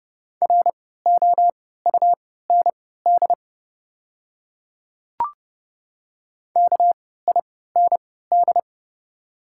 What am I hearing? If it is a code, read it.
Morse code, heard as a keyed tone, about 30 wpm, spaced as for 17 wpm, 690 Hz. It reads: ROUND KIND
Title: 3x Character Spacing - Top 200 Words - 30wpm